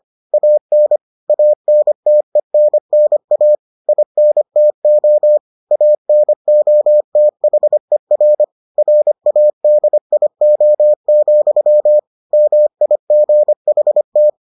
Morse code, heard as a keyed tone, about 25 wpm, 600 Hz. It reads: AN ANTENNA INTO ANOTHER RADIO, MIGHT